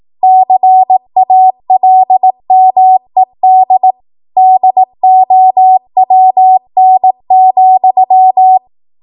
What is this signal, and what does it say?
Morse code, heard as a keyed tone, about 18 wpm, 750 Hz. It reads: CALMED DOWN,
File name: Morse / 18WPM